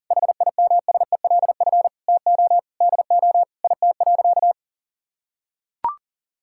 Morse code, heard as a keyed tone, about 40 wpm, 700 Hz. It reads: HIMSELF TO DO IT.
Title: Sentences from Top 1600 Words 40wpm